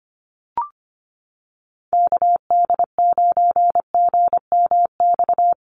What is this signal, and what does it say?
Morse code, heard as a keyed tone, about 25 wpm, 700 Hz. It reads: KD9GMX
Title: USA Call Signs Repeated 3x 25wpm